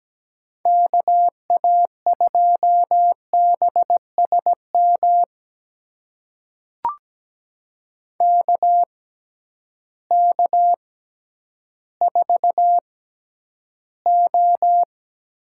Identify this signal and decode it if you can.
Morse code, heard as a keyed tone, about 17 wpm, spaced as for 10 wpm, 700 Hz. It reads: KA2BSM KK4O
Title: USA Call Signs - 6 Characters - 6x Character Spacing - 17wpm